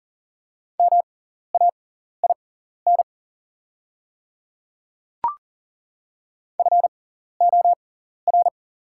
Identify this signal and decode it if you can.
Morse code, heard as a keyed tone, about 40 wpm, spaced as for 14 wpm, 700 Hz. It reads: MAIN FOR
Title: 6x Character Spacing - Top 300 Words - 40wpm